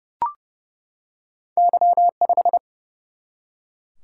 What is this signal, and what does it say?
Morse code, heard as a keyed tone, about 30 wpm, 700 Hz. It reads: Y5
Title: Single-Letter Single-Number 30wpm